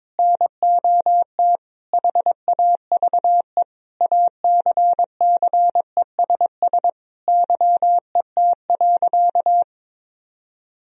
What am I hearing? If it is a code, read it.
Morse code, heard as a keyed tone, about 22 wpm, 700 Hz. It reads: NOT HAVE ACCESS YET.
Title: Sentences from Top 1000 Words - Review Part 4 - 22wpm